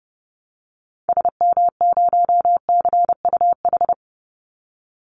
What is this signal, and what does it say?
Morse code, heard as a keyed tone, about 30 wpm, 700 Hz. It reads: SM0CUH
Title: Call Signs Repeated 2x - 30wpm